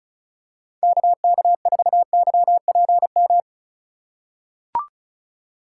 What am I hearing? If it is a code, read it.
Morse code, heard as a keyed tone, about 36 wpm, 690 Hz. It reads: KK4YPM
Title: USA Call Signs - 6 Characters - 6x Character Spacing - 35wpm